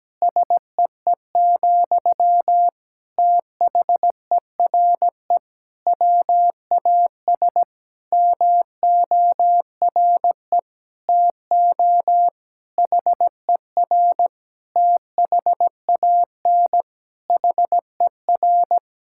Morse code, heard as a keyed tone, about 17 wpm, 700 Hz. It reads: SEE, THERE WAS MORE TO HER THAN HER
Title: Sentences from Top 1000 Words - Review Part 6 - 17wpm